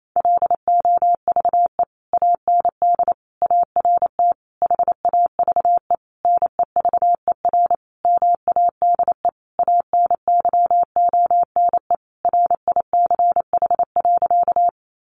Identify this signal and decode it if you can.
Morse code, heard as a keyed tone, about 28 wpm, 700 Hz. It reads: LOVE AND ART HAVE NEVER MADE ANYONE RICH.
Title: Sentences from Top 1100 Words 28wpm